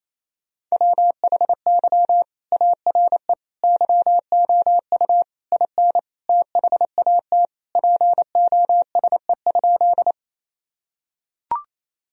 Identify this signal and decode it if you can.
Morse code, heard as a keyed tone, about 28 wpm, 690 Hz. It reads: WHY ARE YOU IN THAT POSE?
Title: Sentences from Top 1000 Words - Review Part 2 - 28wpm